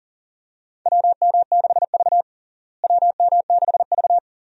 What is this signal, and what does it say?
Morse code, heard as a keyed tone, about 37 wpm, 700 Hz. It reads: WM6V WM6V
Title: USA Call Signs - 4 Characters - Repeated 2x - 40wpm